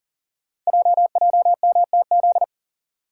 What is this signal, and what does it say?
Morse code, heard as a keyed tone, about 39 wpm, 690 Hz. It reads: JJMTZ
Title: Code Groups - 5 Letters - 40wpm